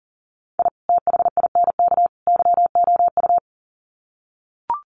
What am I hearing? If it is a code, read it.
Morse code, heard as a keyed tone, about 40 wpm, 700 Hz. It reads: I THINK YOU
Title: Sets of 3 Words 40wpm